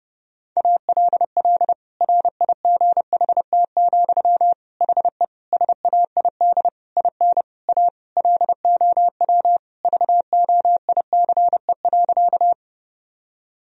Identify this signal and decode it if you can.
Morse code, heard as a keyed tone, about 30 wpm, 700 Hz. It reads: ALL RIGHT, HE SAID IN A LOW VOICE.
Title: Sentences from Top 1000 Words - Review Part 1 - 30wpm